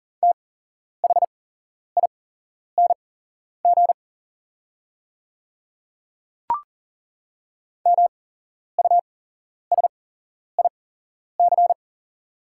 Morse code, heard as a keyed tone, about 40 wpm, spaced as for 11 wpm, 700 Hz. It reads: THING MUSIC
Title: 8x Character Spacing - Top 300 Words - 40wpm